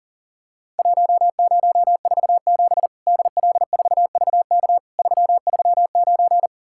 Morse code, heard as a keyed tone, about 40 wpm, 700 Hz. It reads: 1047 DL4VK 339
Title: Worked All Europe DX Contest - QTC Exchange - 40wpm